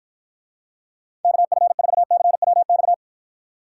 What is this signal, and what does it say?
Morse code, heard as a keyed tone, about 54 wpm, 700 Hz. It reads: KF4CWX